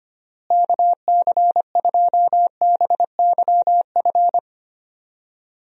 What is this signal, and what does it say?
Morse code, heard as a keyed tone, about 25 wpm, 700 Hz. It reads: KC2BYF
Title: USA Call Signs - 6 Characters - Repeated 3x - 25wpm